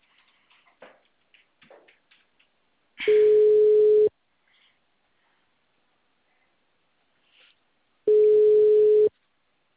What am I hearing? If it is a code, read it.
Ringback tone (Europe, 425 Hz).